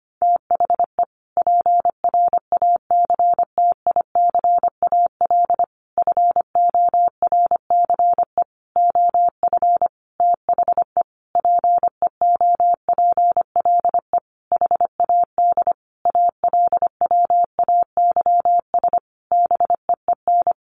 Morse code, heard as a keyed tone, about 25 wpm, 700 Hz. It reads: THE PRACTICAL FORCE OF THE PEOPLE HAD ALWAYS BEEN
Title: Sentences from Top 1400 Words 25wpm